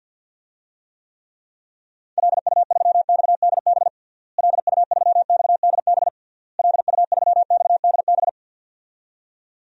Morse code, heard as a keyed tone, about 52 wpm, 700 Hz. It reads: LU3XDB LU3XDB LU3XDB
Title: DXCC-100 Call Signs - 6 Characters - Repeated 3x - 50wpm